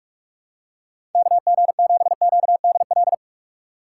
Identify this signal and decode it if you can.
Morse code, heard as a keyed tone, about 46 wpm, 700 Hz. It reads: KG7QDL
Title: USA Call Signs - 6 Characters - 4x Character Spacing - 45wpm